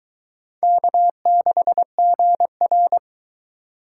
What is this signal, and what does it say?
Morse code, heard as a keyed tone, about 21 wpm, 700 Hz. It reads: K6GR